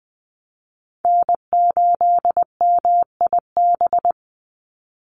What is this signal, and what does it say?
Morse code, heard as a keyed tone, about 20 wpm, 700 Hz. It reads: N8MIB
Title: USA Call Signs 20wpm